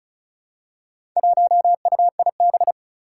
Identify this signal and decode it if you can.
Morse code, heard as a keyed tone, about 36 wpm, 700 Hz. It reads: 1UIB
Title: Four Challenging Letter-Number Combinations 35wpm